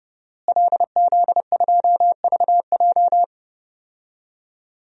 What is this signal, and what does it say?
Morse code, heard as a keyed tone, about 30 wpm, 700 Hz. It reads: LZ2VJ
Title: DXCC-100 Call Signs - 5 Characters - Repeated 2x - 30wpm